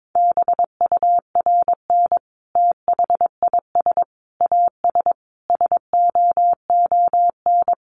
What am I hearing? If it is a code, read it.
Morse code, heard as a keyed tone, about 22 wpm, 700 Hz. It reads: BURN THIS AS SOON
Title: Sentences from Top 1300 Words 22wpm